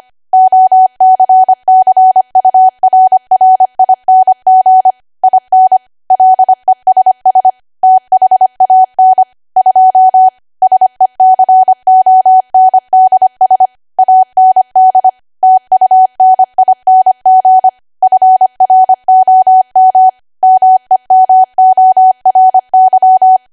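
Morse code, heard as a keyed tone, about 25 wpm, 750 Hz. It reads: OCCURRING IN LESS THAN 2 SECONDS AND TUNING FROM MEMORY
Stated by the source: ARRL 25 WPM code practice archive.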